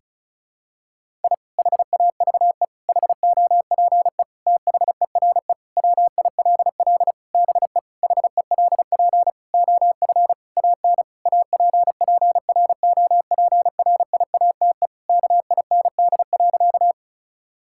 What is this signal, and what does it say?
Morse code, heard as a keyed tone, about 36 wpm, 700 Hz. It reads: I HAVE HOPE THERE WILL BE HELP OF AN APPROPRIATE KIND.
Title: Sentences from Top 1000 Words - Review Part 5 - 35wpm